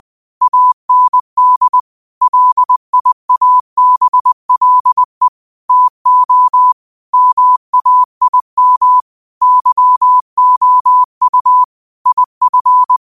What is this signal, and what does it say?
Morse code, heard as a keyed tone, about 20 wpm, 1000 Hz. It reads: AND LIABLE TO MAIM YOU IF